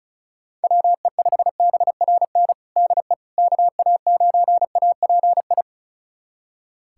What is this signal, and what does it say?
Morse code, heard as a keyed tone, about 36 wpm, 700 Hz. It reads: WE5BRN DE KA9API
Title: USA Callsign DE Callsign - 6 Characters - 35wpm